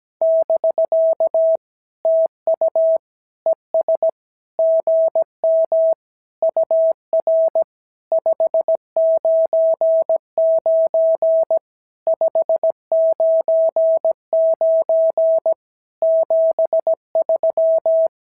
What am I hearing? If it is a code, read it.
Morse code, heard as a keyed tone, about 17 wpm, 650 Hz. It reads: <BK> TU ES GM UR 599 599 73
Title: SOTA QSO Practice 17wpm